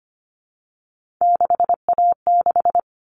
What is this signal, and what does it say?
Morse code, heard as a keyed tone, about 25 wpm, 700 Hz. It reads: 6A6
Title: All Cut Numbers - 3-Digit 25wpm